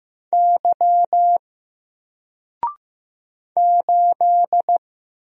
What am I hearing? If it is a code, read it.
Morse code, heard as a keyed tone, about 15 wpm, 700 Hz. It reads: Y 8
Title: Single Letter-Number 15wpm